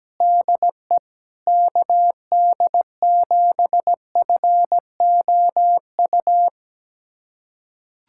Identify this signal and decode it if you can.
Morse code, heard as a keyed tone, about 17 wpm, 700 Hz. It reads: DE KD7FOU
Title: USA Callsign DE Callsign 17wpm